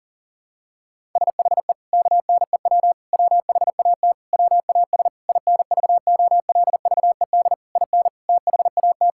Morse code, heard as a keyed tone, about 40 wpm, 700 Hz. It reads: SHE KNEW WHAT WAS INVOLVED IN THAT
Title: Sentences from Top 1000 Words - Review Part 5 - 40wpm